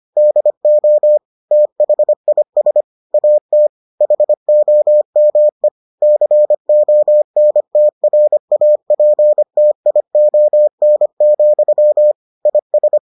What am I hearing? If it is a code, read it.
Morse code, heard as a keyed tone, about 25 wpm, 600 Hz. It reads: DO THIS AT HOME CONTRAPTION, IS